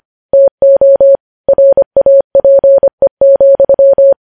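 Morse code, heard as a keyed tone, about 25 wpm, 550 Hz. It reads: TO RAPE,
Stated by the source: Morse practice of news headlines at 25wpm.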